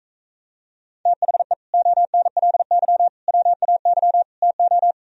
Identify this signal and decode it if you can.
Morse code, heard as a keyed tone, about 45 wpm, 700 Hz. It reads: THE ONLY WAY TO